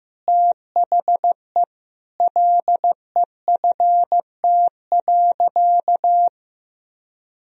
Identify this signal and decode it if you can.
Morse code, heard as a keyed tone, about 16 wpm, 700 Hz. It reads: THE LEFT.
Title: Sentences from Top 1000 Words - Review Part 4 - 15wpm